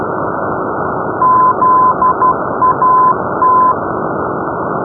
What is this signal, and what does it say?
Morse code, heard as a keyed tone, about 12 wpm, 1020 Hz, with noise. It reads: ZAT
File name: europe / grc / grc-334-zat-np-zakynthos-grc-rb-2015